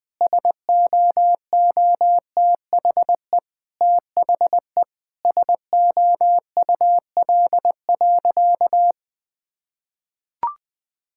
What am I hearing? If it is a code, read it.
Morse code, heard as a keyed tone, about 20 wpm, 700 Hz. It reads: SOOTHE THE SOUL.
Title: Difficult Sentences with Hints - 20wpm